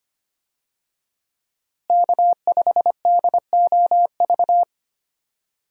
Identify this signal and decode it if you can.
Morse code, heard as a keyed tone, about 25 wpm, 700 Hz. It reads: K5DOV
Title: USA Call Signs - 5 Characters - Repeated 3x - 25wpm